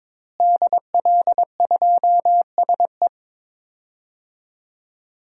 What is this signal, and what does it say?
Morse code, heard as a keyed tone, about 21 wpm, 700 Hz. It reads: DL2SE
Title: DXCC-100 Call Signs - 5 Characters - Repeated 2x - 22wpm